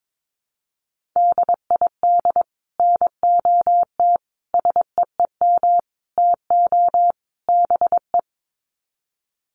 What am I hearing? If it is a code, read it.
Morse code, heard as a keyed tone, about 22 wpm, 700 Hz. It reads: DID NOT SEEM TO BE
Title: Sets of 5 Words Encore 22wpm